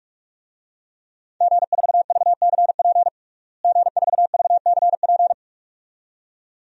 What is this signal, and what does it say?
Morse code, heard as a keyed tone, about 46 wpm, 700 Hz. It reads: G4VCP G4VCP
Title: DXCC-100 Call Signs - 5 Characters - Repeated 2x - 45wpm